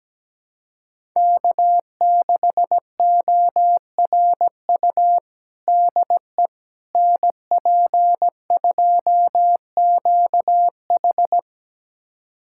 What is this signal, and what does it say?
Morse code, heard as a keyed tone, about 17 wpm, 700 Hz. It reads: K6ORU DE NP2QH
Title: USA Callsign DE Callsign - 5 Characters - 17wpm